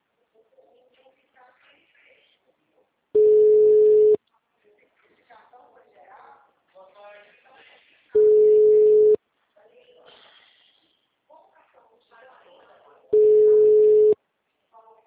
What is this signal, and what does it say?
Ringback tone (Europe, 425 Hz), with speech.
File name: out- -2014- - - .51651